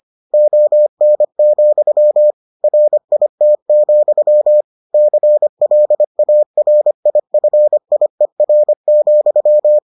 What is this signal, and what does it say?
Morse code, heard as a keyed tone, about 25 wpm, 600 Hz. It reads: ON, RIT, CLARIFIER,